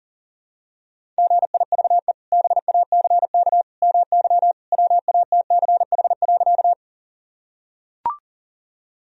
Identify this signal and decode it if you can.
Morse code, heard as a keyed tone, about 40 wpm, 700 Hz. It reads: GIVE BACK MY WATCH.
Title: Sentences from Top 1000 Words - Review Part 1 - 40wpm